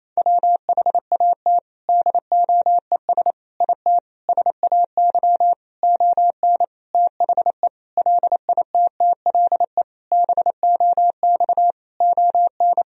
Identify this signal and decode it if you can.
Morse code, heard as a keyed tone, about 28 wpm, 700 Hz. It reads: WHAT DOES IT SAY ON THE LITTLE BOX ON
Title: Sentences from Top 1000 Words - Review Part 2 - 28wpm